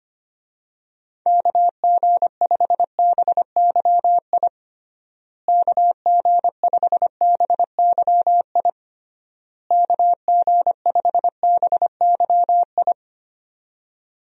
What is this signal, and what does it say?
Morse code, heard as a keyed tone, about 25 wpm, 700 Hz. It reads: KG5BYI KG5BYI KG5BYI
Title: USA Call Signs - 6 Characters - Repeated 3x - 25wpm